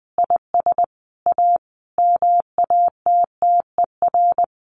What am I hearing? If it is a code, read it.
Morse code, heard as a keyed tone, about 20 wpm, 700 Hz. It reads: IS A MATTER
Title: Sentences from Top 1100 Words 20wpm